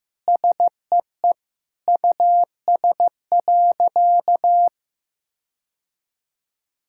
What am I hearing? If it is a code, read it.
Morse code, heard as a keyed tone, about 16 wpm, 700 Hz. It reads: SEE US.
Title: Sentences from Top 1600 Words 15wpm